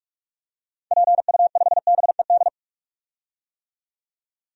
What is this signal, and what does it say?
Morse code, heard as a keyed tone, about 46 wpm, 700 Hz. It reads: PU5BED
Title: DXCC-100 Call Signs - 6 Characters - 45wpm